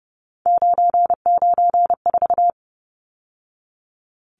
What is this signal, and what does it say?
Morse code, heard as a keyed tone, about 30 wpm, 700 Hz. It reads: 994
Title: Numbers - 3-Digit 30wpm